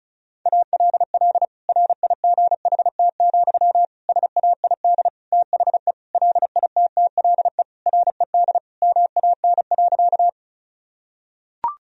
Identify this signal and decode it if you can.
Morse code, heard as a keyed tone, about 36 wpm, 700 Hz. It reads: ALL RIGHT, SAID THE LITTLE RED MAN.
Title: Sentences from Top 1000 Words - Review Part 2 - 35wpm